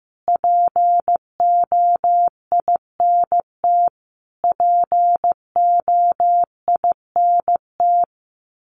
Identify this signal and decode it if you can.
Morse code, heard as a keyed tone, about 15 wpm, 700 Hz. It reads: POINT POINT
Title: Top 500 Words Repeated 15wpm